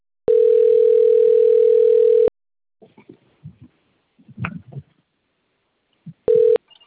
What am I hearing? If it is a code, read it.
Ringback tone (North America), 440 + 480 Hz.